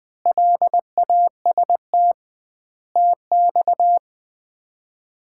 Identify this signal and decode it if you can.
Morse code, heard as a keyed tone, about 20 wpm, 700 Hz. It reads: LAST TX
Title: Common QSO Phrases - 2x Word-Spacing - 20wpm